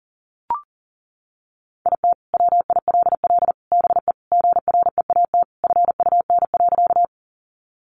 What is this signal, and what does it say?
Morse code, heard as a keyed tone, about 37 wpm, 700 Hz. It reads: IT WILL BE GREAT FUN.
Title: Sentences from Top 700 Words 40wpm